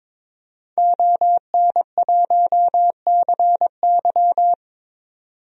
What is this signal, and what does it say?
Morse code, heard as a keyed tone, about 22 wpm, 700 Hz. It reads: ON1CY
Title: DXCC-100 Call Signs - 5 Characters - 22wpm